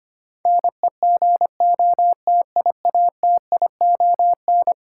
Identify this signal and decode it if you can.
Morse code, heard as a keyed tone, about 25 wpm, 700 Hz. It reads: NEGOTIATION
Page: morse-code-ninja-practice-25wpm